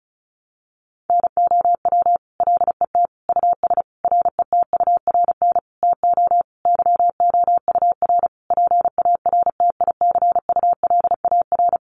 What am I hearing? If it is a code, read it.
Morse code, heard as a keyed tone, about 34 wpm, 700 Hz. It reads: NOW LET US RETURN TO YOUR PARTICULAR
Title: Sentences from Top 800 Words 35wpm